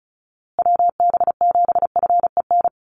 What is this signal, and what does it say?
Morse code, heard as a keyed tone, about 34 wpm, 700 Hz. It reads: WB7FEN
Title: USA Call Signs Repeated 3x 35wpm